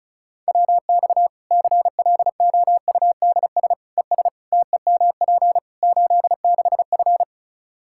Morse code, heard as a keyed tone, about 36 wpm, 700 Hz. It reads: WX CLOUDS ES TEMP 86F
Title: Practice QSO - Exchange 2 - Info - 35wpm